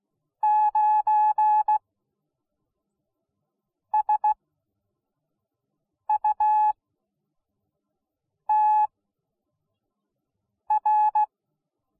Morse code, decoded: 9SUTR

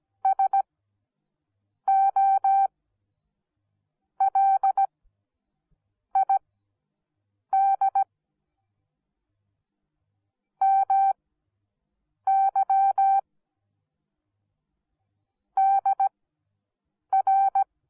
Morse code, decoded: SOLID MY DR

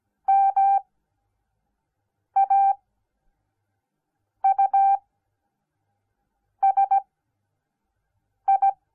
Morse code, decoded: MAUSI

